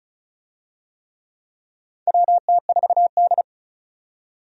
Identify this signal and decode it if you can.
Morse code, heard as a keyed tone, about 36 wpm, 700 Hz. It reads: WT4D
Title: USA Call Signs - 4 Characters - 35wpm